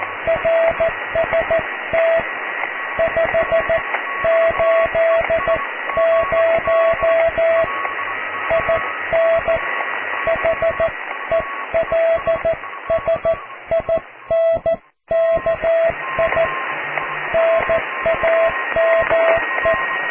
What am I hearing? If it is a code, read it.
Morse code, heard as a keyed tone, about 14 wpm, 660 Hz, with radio noise. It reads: RST 580 IN HELSINKI NAME
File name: cw / QSO+QRM